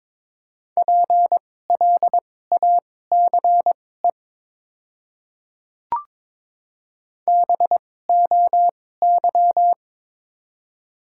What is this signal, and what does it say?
Morse code, heard as a keyed tone, about 21 wpm, spaced as for 16 wpm, 700 Hz. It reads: PLACE BOY
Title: 2x Character Spacing - Top 200 Words - 22wpm